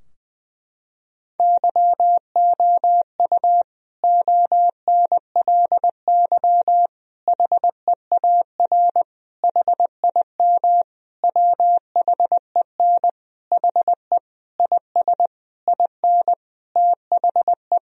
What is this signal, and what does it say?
Morse code, heard as a keyed tone, about 20 wpm, 700 Hz. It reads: YOU ONLY HEAR HIM WHEN HE IS IN THE
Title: Sentences from Top 1000 Words - Review Part 2 - 20wpm